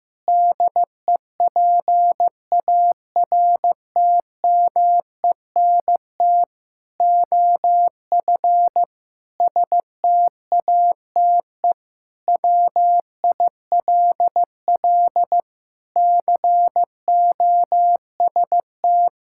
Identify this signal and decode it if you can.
Morse code, heard as a keyed tone, about 15 wpm, 700 Hz. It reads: DEPARTMENT OF STATE WILL COST
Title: Sentences from Top 1000 Words - Review Part 5 - 15wpm